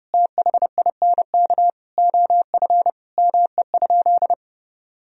Morse code, heard as a keyed tone, about 30 wpm, 700 Hz. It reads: THINK OF ME?
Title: Sentences from Top 1000 Words - Review Part 3 - 30wpm